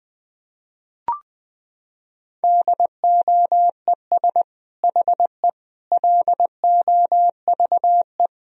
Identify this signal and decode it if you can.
Morse code, heard as a keyed tone, about 20 wpm, 700 Hz. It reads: DOES HE LOVE
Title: Sentences from Top 400 Words and Abbreviations 20wpm